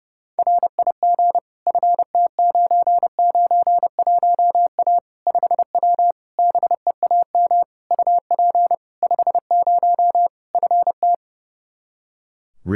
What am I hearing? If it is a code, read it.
Morse code, heard as a keyed tone, about 30 wpm, 700 Hz. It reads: RIG FT991A 5W BEAM UP 50 FT
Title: Practice QSO - Exchange 2 - Info - 30wpm